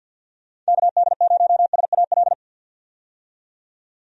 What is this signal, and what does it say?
Morse code, heard as a keyed tone, about 46 wpm, 700 Hz. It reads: KD0SAL
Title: USA Call Signs - 6 Characters - 50wpm